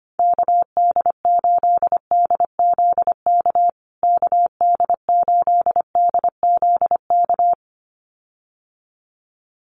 Morse code, heard as a keyed tone, about 25 wpm, 700 Hz. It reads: KD8DZK KD8DZK
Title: USA Call Signs Repeated 2x 25wpm